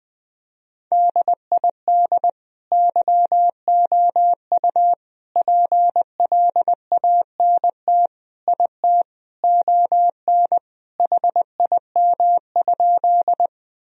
Morse code, heard as a keyed tone, about 20 wpm, 700 Hz. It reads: DID YOU PLANT IT ON HIM?
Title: Sentences from Top 1000 Words - Review Part 1 - 20wpm